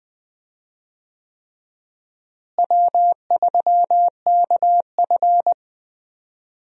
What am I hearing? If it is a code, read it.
Morse code, heard as a keyed tone, about 19 wpm, 700 Hz. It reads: W3KF
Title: USA Call Signs - 4 Characters - Repeated 3x - 20wpm